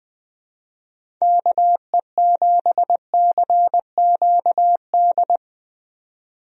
Morse code, heard as a keyed tone, about 20 wpm, 700 Hz. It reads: KE7CQD
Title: USA Call Signs - 6 Characters - 20wpm